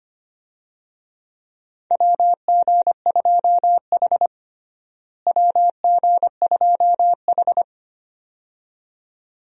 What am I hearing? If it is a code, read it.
Morse code, heard as a keyed tone, about 25 wpm, 700 Hz. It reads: WG2H WG2H
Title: USA Call Signs - 4 Characters - Repeated 2x - 25wpm